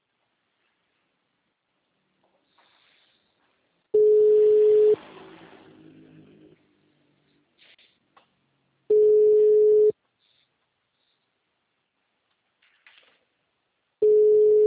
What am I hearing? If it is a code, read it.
Ringback tone (Europe, 425 Hz).
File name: out- -2004- - - .71378